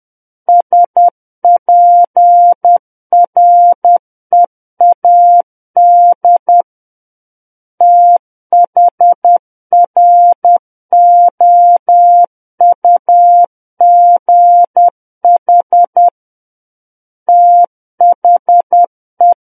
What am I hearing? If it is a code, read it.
Morse code, heard as a keyed tone, about 10 wpm, 700 Hz. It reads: SPREAD THROUGH THE
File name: Book One / 01-Chapter 1 - 10 WPM